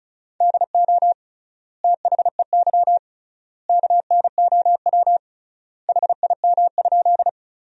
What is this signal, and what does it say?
Morse code, heard as a keyed tone, about 34 wpm, 700 Hz. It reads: DO THEY KNOW HIM?
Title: 3x Word-Spacing - Sentences from Top 100 Words - 35wpm